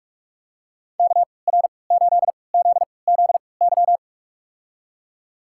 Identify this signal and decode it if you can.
Morse code, heard as a keyed tone, about 42 wpm, spaced as for 25 wpm, 700 Hz. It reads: KR8ZZY
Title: USA Call Signs - 6 Characters - 3x Character Spacing - 45wpm